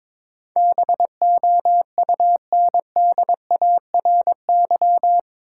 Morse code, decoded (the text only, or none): BOUNDARY